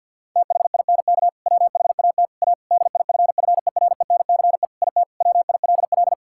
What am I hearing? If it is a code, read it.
Morse code, heard as a keyed tone, about 50 wpm, 700 Hz. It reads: THINK WHAT A DIFFERENCE IT WILL
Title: Sentences from Top 1000 Words - Review Part 5 - 50wpm